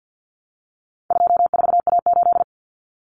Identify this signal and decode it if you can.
Morse code, heard as a keyed tone, about 50 wpm, 700 Hz. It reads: 24A8
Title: Common Cut Numbers - 4-Digit 50wpm